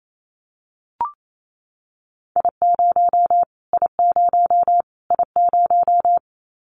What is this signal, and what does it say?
Morse code, heard as a keyed tone, about 28 wpm, 700 Hz. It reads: I0 I0 I0
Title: Callsign Prefixes for Top 25 DXCC - Repeated 3x - 28wpm